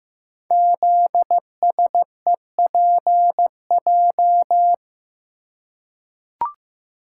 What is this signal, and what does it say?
Morse code, heard as a keyed tone, about 15 wpm, 700 Hz. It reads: ZSEPJ
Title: Code Groups - 5 Letters - 15wpm